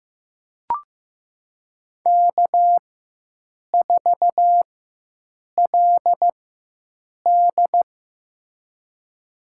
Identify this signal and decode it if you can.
Morse code, heard as a keyed tone, about 15 wpm, spaced as for 7 wpm, 700 Hz. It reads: K4LD
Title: USA Call Signs - 4 Characters - 4x Character Spacing - 15wpm